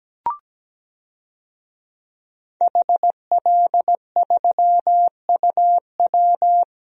Morse code, heard as a keyed tone, about 17 wpm, 700 Hz. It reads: HL3UW
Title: DXCC-100 Call Signs - 5 Characters - Repeated 2x - 17wpm